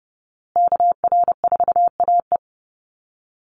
Morse code, heard as a keyed tone, about 28 wpm, 700 Hz. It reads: KR4AE
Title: Call Signs Repeated 2x - 30wpm